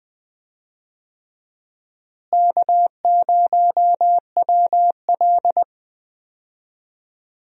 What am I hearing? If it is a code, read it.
Morse code, heard as a keyed tone, about 20 wpm, 700 Hz. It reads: K0WL
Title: USA Call Signs - 4 Characters - Repeated 3x - 20wpm